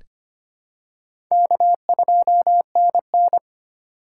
Morse code, heard as a keyed tone, about 25 wpm, 700 Hz. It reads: K2NN
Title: USA Call Signs - 3x Character Spacing - 25wpm